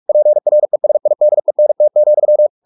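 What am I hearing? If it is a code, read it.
Morse code, heard as a keyed tone, about 44 wpm, 600 Hz. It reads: PRESIDENT,
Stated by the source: Morse practice at 45 wpm, the novel Never by Ken Follett in English, from CW med Gnister.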